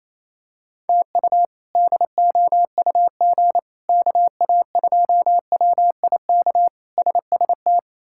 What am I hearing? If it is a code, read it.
Morse code, heard as a keyed tone, about 28 wpm, 700 Hz. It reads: TU DOUG KA2WIK SST
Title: K1USN SST QSO Practice - 28wpm